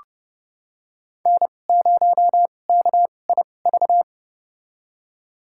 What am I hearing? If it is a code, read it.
Morse code, heard as a keyed tone, about 31 wpm, spaced as for 22 wpm, 700 Hz. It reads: N0KIV
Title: USA Call Signs - 5 Characters - 2x Character Spacing - 30wpm